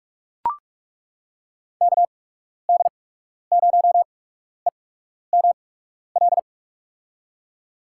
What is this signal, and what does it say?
Morse code, heard as a keyed tone, about 42 wpm, spaced as for 12 wpm, 700 Hz. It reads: KD0EML
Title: USA Call Signs - 6 Characters - 8x Character Spacing - 45wpm